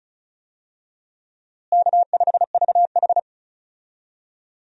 Morse code, heard as a keyed tone, about 36 wpm, 700 Hz. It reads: K5VH
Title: USA Call Signs - 4 Characters - Repeated 3x - 35wpm